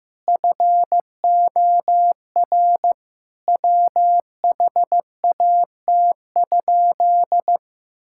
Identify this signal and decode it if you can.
Morse code, heard as a keyed tone, about 14 wpm, 700 Hz. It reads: FOR WHAT?
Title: Sentences from Top 1000 Words - Review Part 6 - 15wpm